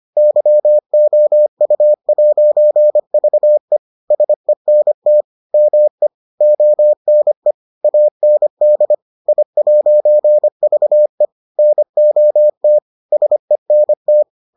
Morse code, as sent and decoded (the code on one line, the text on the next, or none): -.-- --- ..- .----. ...- . / ... -. - / -- . / --- -. / .- -. -.. / .. .----. ...- . / -. --- - / ... -. -
YOU'VE SENT ME ONE AND I'VE NOT SENT